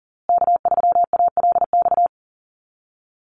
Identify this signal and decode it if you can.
Morse code, heard as a keyed tone, about 38 wpm, 700 Hz. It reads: K3ALX
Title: USA Call Signs Repeated 3x 40wpm